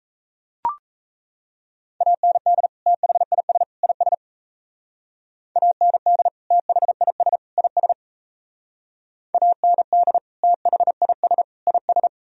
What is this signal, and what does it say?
Morse code, heard as a keyed tone, about 34 wpm, 700 Hz. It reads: AND THIS IS AND THIS IS AND THIS IS
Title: Speed-Racing - Sets of 3 Words 28wpm